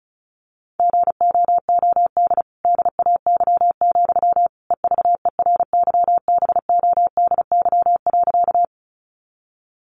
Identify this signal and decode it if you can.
Morse code, heard as a keyed tone, about 34 wpm, 700 Hz. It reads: GOOD DAY, EVERYBODY.
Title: Sentences from Top 1400 Words 35wpm